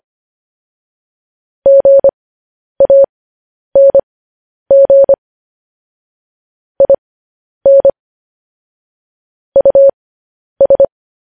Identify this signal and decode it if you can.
Morse code, heard as a keyed tone, about 25 wpm, spaced as for 10 wpm, 550 Hz. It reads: GANG IN US
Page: news headlines in morse code at 10 wpm